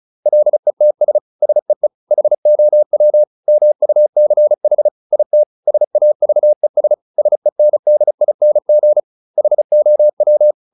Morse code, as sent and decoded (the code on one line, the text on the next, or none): .-.. - ... / ... . / .... --- .-- / -- ..- -.-. .... / .. - / ... .- ...- . ... / ... -. -.. .. -. --. / .... --- .--
LETS SEE HOW MUCH IT SAVES SENDING HOW